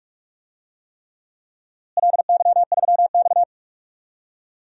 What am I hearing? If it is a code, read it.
Morse code, heard as a keyed tone, about 46 wpm, 700 Hz. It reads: LY3X
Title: DXCC-100 Call Signs - 4 Characters - Repeated 3x - 45wpm